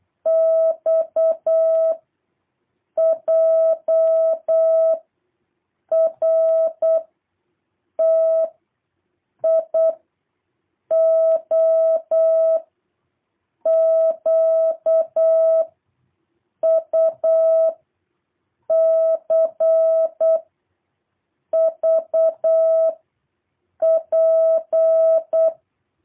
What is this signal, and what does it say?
Morse code, heard as a keyed tone, about 8 wpm, 660 Hz. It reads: X J R T I O Q U C V P
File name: jh8cbh / 2023cw